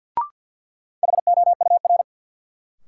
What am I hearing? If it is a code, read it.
Morse code, heard as a keyed tone, about 53 wpm, 700 Hz. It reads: HOUR
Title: Sentences from Top 400 Words - New Word Drill - 50wpm